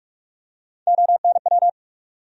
Morse code, heard as a keyed tone, about 42 wpm, 700 Hz. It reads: ONW
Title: USA Call Sign Postfixes - 3 Letters 45wpm